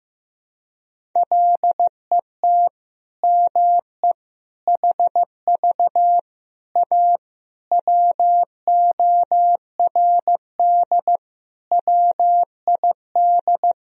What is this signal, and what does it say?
Morse code, heard as a keyed tone, about 16 wpm, 700 Hz. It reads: LET ME HV A WORD WID
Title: Sentences from Top 300 Words and Abbreviations 15wpm